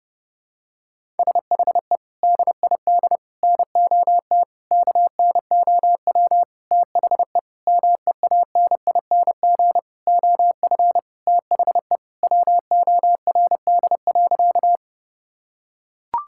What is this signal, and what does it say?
Morse code, heard as a keyed tone, about 30 wpm, 700 Hz. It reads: SHE DID NOT KNOW THE MEANING OF THE WORD.
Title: Sentences from Top 1000 Words - Review Part 6 - 30wpm